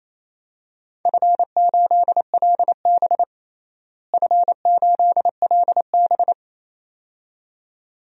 Morse code, heard as a keyed tone, about 27 wpm, 700 Hz. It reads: F8LB F8LB
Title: DXCC-100 Call Signs - 4 Characters - Repeated 2x - 28wpm